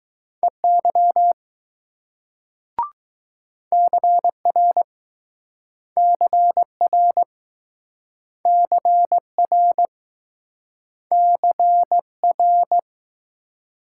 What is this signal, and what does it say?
Morse code, heard as a keyed tone, about 20 wpm, 700 Hz. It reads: EY CR CR CR CR